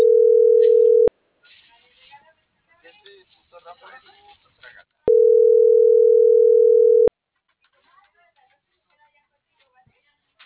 Ringback tone (North America), 440 + 480 Hz.